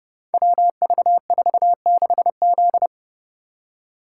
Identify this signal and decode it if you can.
Morse code, heard as a keyed tone, about 29 wpm, 700 Hz. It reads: WV46Z
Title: Code Groups - 5 Letter:Numbers - 30wpm